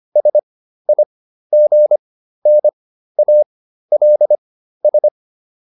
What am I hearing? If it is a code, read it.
Morse code, heard as a keyed tone, about 25 wpm, spaced as for 13 wpm, 600 Hz. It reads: SIGNALS